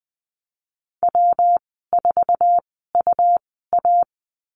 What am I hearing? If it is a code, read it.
Morse code, heard as a keyed tone, about 20 wpm, spaced as for 14 wpm, 700 Hz. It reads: W4UA
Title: 193 Lesson 31g - Letter v - Call Signs 20wpm